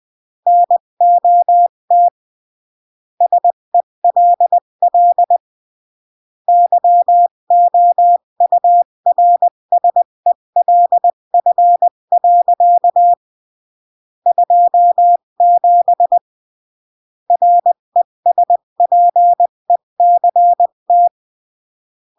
Morse code, heard as a keyed tone, about 20 wpm, 700 Hz. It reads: NOT SELL YOURSELF. 27 RESPECT
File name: CW / jodorowsky / 82 commandments 20-15